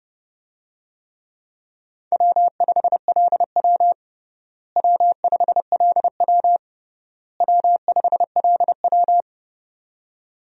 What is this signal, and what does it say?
Morse code, heard as a keyed tone, about 30 wpm, 700 Hz. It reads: W5LW W5LW W5LW
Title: USA Call Signs - 4 Characters - Repeated 3x - 30wpm